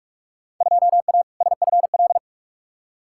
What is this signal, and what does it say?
Morse code, heard as a keyed tone, about 46 wpm, 700 Hz. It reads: 2A SFL
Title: Field Day Exchanges Encore - 45wpm